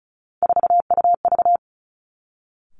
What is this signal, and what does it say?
Morse code, heard as a keyed tone, about 36 wpm, 700 Hz. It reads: 4UV
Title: Three Challenging Letter-Number Combinations 35wpm